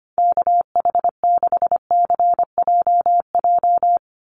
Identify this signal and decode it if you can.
Morse code, heard as a keyed tone, about 25 wpm, 700 Hz. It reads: KH6CJJ
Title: Call Signs Repeated 2x - 25wpm